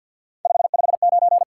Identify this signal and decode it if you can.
Morse code, heard as a keyed tone, about 61 wpm, 700 Hz. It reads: 559
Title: Speed-Racing Top 180 QSO Elements 50wpm